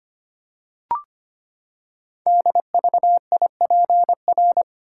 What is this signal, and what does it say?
Morse code, heard as a keyed tone, about 25 wpm, 700 Hz. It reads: DVIPR
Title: Code Groups - 5 Letters - 25wpm